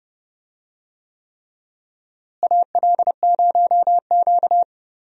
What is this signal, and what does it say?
Morse code, heard as a keyed tone, about 31 wpm, 700 Hz. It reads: AL0Q
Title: USA Call Signs - 4 Characters - Repeated 3x - 30wpm